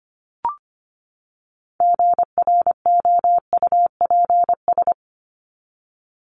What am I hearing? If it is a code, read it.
Morse code, heard as a keyed tone, about 25 wpm, 700 Hz. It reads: GROUPS
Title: ES:S Suffix - Top 100 Words with Base Words 25wpm